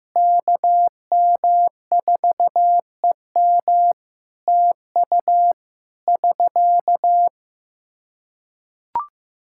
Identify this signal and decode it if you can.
Morse code, heard as a keyed tone, about 16 wpm, 700 Hz. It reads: KM4EM TU <SK>
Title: Practice QSO - Exchange 4 - Ending - 15wpm